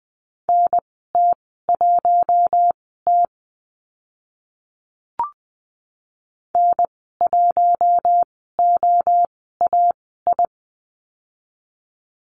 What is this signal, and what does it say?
Morse code, heard as a keyed tone, about 20 wpm, spaced as for 14 wpm, 700 Hz. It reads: NT1T N1OAI
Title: 042 Lesson 8g - Number 1 - Call Signs 20wpm